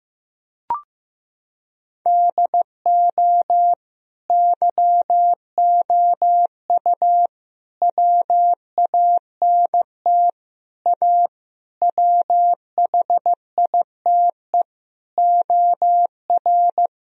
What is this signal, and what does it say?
Morse code, heard as a keyed tone, about 16 wpm, 700 Hz. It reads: DO YOU WANT A WHITE OR
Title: Sentences from Top 1000 Words - Review Part 2 - 15wpm